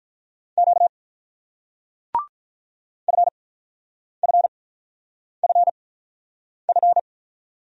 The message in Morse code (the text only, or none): Q FFFF